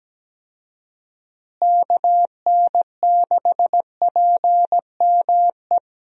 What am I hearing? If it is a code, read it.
Morse code, heard as a keyed tone, about 17 wpm, 700 Hz. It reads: KN6PME
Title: USA Call Signs - 6 Characters - Repeated 2x - 17wpm